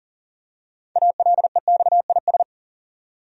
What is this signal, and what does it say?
Morse code, heard as a keyed tone, about 40 wpm, 700 Hz. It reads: ALEXIS
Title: USA Common Names - 40wpm